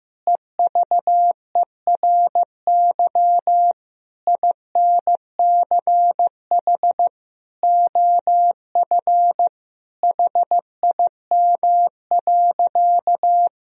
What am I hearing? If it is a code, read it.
Morse code, heard as a keyed tone, about 15 wpm, 700 Hz. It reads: EVERY INCH OF HIM.
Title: Sentences from Top 1000 Words - Review Part 3 - 15wpm